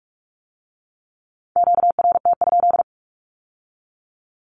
Morse code, heard as a keyed tone, about 46 wpm, 700 Hz. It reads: QRT?
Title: Q-Codes with Spoken Meaning - 45wpm